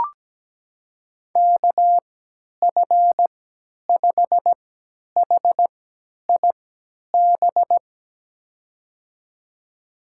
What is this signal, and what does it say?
Morse code, heard as a keyed tone, about 17 wpm, spaced as for 10 wpm, 700 Hz. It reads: KF5HIB
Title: USA Call Signs - 6 Characters - 3x Character Spacing - 17wpm